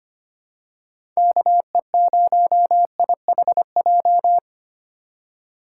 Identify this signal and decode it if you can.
Morse code, heard as a keyed tone, about 25 wpm, 700 Hz. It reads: KE0IHJ
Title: USA Call Signs - 6 Characters - 4x Character Spacing - 25wpm